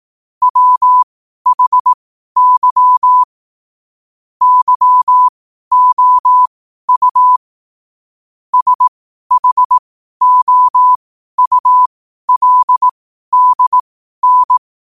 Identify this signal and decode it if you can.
Morse code, heard as a keyed tone, about 18 wpm, spaced as for 12 wpm, 1000 Hz. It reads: WHY YOU SHOULDN
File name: qotdfiles / 12wpm